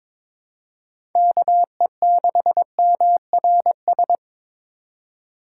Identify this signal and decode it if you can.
Morse code, heard as a keyed tone, about 21 wpm, 700 Hz. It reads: KE6MRS
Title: USA Call Signs - 6 Characters - 2x Character Spacing - 22wpm